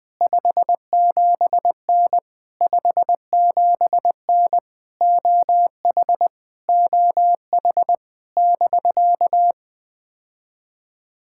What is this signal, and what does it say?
Morse code, heard as a keyed tone, about 20 wpm, 700 Hz. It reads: 57N 57N OH OH <BK>
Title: POTA QSO Practice - 20wpm